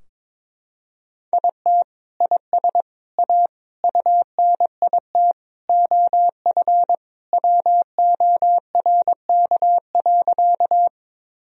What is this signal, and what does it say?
Morse code, heard as a keyed tone, about 22 wpm, 700 Hz. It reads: IT IS A UNIT OF WORK.
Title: Sentences from Top 1000 Words - Review Part 2 - 22wpm